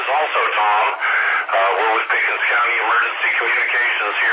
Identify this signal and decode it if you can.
Emergency Alert System (SAME) tones, heard as harsh buzzing bursts, with speech. It is heard from 1.0 to 3.7 s.